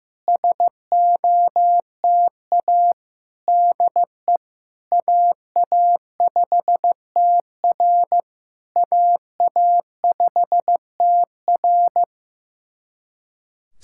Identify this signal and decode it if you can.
Morse code, heard as a keyed tone, about 15 wpm, 700 Hz. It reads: SOTA DE AA5TR AA5TR
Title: SOTA QSO Practice 15wpm